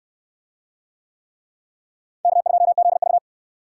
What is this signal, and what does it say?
Morse code, heard as a keyed tone, about 63 wpm, 700 Hz. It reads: K2ZV